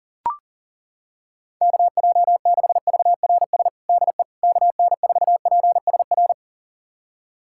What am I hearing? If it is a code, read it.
Morse code, heard as a keyed tone, about 40 wpm, 700 Hz. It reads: KJ6VRS DE KN4PSR
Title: USA Callsign DE Callsign - 6 Characters - 40wpm